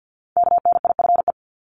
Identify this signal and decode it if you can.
Morse code, heard as a keyed tone, about 48 wpm, 700 Hz. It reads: KNIFE
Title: Top 1751-2000 Words 50wpm